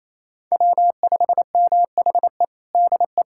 Morse code, heard as a keyed tone, about 28 wpm, 700 Hz. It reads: W5MHE DE
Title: USA Callsign DE Callsign - 5 Characters - 28wpm